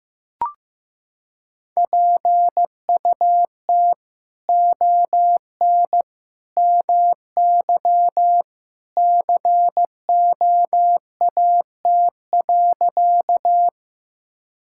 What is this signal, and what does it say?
Morse code, heard as a keyed tone, about 15 wpm, 700 Hz. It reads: PUT ON MY COAT.
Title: Sentences from Top 1600 Words 15wpm